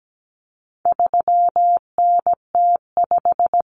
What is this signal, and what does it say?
Morse code, heard as a keyed tone, about 17 wpm, 700 Hz. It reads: 3NT5
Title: Common Cut Numbers - 4-Digit 17wpm